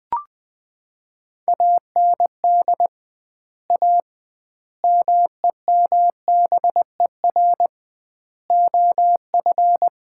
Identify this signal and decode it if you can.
Morse code, heard as a keyed tone, about 20 wpm, 700 Hz. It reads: AND A MEMBER OF
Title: 2x Word-Spacing - Sets of 5 Words - 20wpm